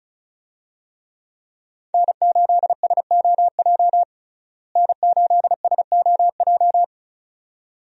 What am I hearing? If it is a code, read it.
Morse code, heard as a keyed tone, about 34 wpm, 700 Hz. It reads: N8SOJ N8SOJ
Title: USA Call Signs - 5 Characters - Repeated 2x - 35wpm